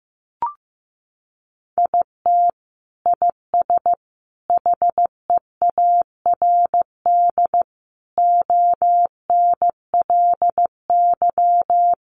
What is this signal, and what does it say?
Morse code, heard as a keyed tone, about 15 wpm, 700 Hz. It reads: IT IS HEARD ONLY
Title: Sentences from Top 1200 Words 15wpm